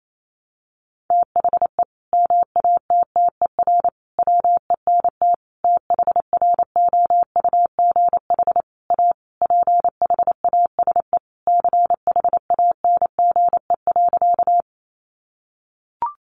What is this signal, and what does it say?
Morse code, heard as a keyed tone, about 28 wpm, 700 Hz. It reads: THE MATTER WENT THROUGH A PHASE CHANGE.
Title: Sentences from Top 1200 Words 28wpm